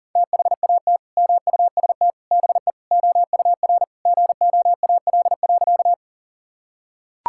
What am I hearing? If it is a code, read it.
Morse code, heard as a keyed tone, about 37 wpm, 700 Hz. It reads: THAT MUST BE OUR GOAL.
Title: Sentences from Top 1000 Words - Review Part 5 - 40wpm